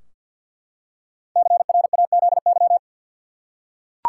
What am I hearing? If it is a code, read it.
Morse code, heard as a keyed tone, about 52 wpm, 700 Hz. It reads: CRAZY
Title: Five Letter Words - 50wpm